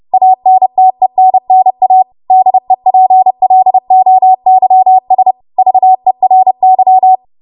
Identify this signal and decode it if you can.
Morse code, heard as a keyed tone, about 30 wpm, 750 Hz. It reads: ANTENNA DEPLOYS VERY
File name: Morse / 30WPM